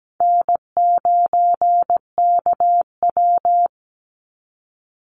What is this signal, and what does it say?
Morse code, heard as a keyed tone, about 17 wpm, 700 Hz. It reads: N9KW
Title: NA SOTA Chaser Callsigns 17wpm